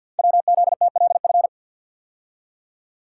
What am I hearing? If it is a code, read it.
Morse code, heard as a keyed tone, about 48 wpm, 700 Hz. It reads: WZTLF
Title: Code Groups - 5 Letters - 50wpm